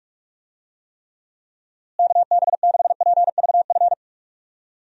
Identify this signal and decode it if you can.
Morse code, heard as a keyed tone, about 46 wpm, 700 Hz. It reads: KB6PVF
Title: USA Call Signs - 6 Characters - Repeated 2x - 45wpm